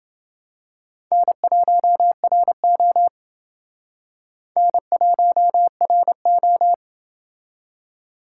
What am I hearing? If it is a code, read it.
Morse code, heard as a keyed tone, about 28 wpm, 700 Hz. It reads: N1RO N1RO